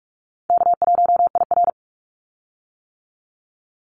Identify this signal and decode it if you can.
Morse code, heard as a keyed tone, about 44 wpm, 700 Hz. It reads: K1IR
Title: Call Signs - 45wpm